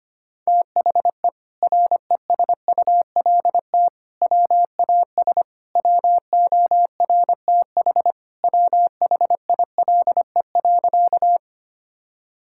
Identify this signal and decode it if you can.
Morse code, heard as a keyed tone, about 25 wpm, 700 Hz. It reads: THE RESULT WAS WORTH WHILE.
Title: Sentences from Top 1000 Words - Review Part 5 - 25wpm